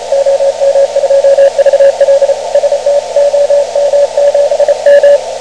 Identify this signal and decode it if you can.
Morse code, heard as a keyed tone, about 32 wpm, 560 Hz, with radio noise. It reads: OM2VL ST OM7M
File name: om7m / OM2VLaOM7M7MHz0300Z